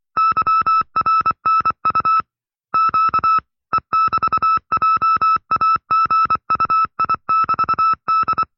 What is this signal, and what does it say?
Morse code, heard as a keyed tone, about 24 wpm, 1310 Hz. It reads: YRNU Q E<BT>JAGUI<BT>D
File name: FreeFiles / CW-practice / 20WPM-side2